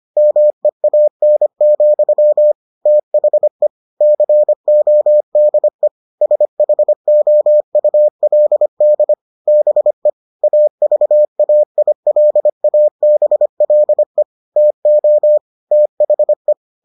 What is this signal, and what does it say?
Morse code, heard as a keyed tone, about 25 wpm, 600 Hz. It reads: MEAN, THE CODE SHOULD BE AVAILABLE TO THE